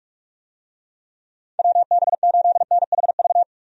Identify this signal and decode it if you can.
Morse code, heard as a keyed tone, about 45 wpm, 700 Hz. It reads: WB8NHV
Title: USA Call Signs - 6 Characters - 45wpm